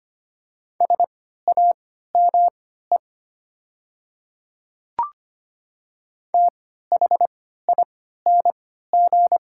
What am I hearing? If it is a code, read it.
Morse code, heard as a keyed tone, about 25 wpm, spaced as for 14 wpm, 690 Hz. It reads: SAME THING
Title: 3x Character Spacing - Top 200 Words - 25wpm